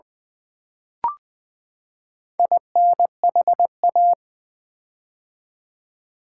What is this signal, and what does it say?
Morse code, heard as a keyed tone, about 20 wpm, 700 Hz. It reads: INHA
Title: Build A Word - 4 Letters At A Time - 10-Letter Words - 20wpm